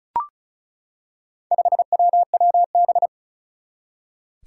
Morse code, heard as a keyed tone, about 34 wpm, 700 Hz. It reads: 5WWB
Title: Four Challenging Letter-Number Combinations 35wpm